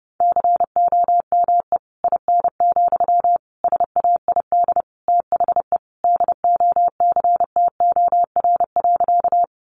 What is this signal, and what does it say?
Morse code, heard as a keyed tone, about 30 wpm, 700 Hz. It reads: COME IN, SAID THE DOCTOR.
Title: Sentences from Top 1500 Words 30wpm